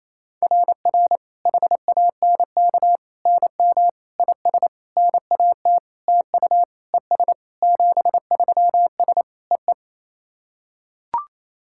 Morse code, heard as a keyed tone, about 28 wpm, 700 Hz. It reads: RR HANK NM IS NAT TU ES 73S EE
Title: SOTA QSO Practice 28wpm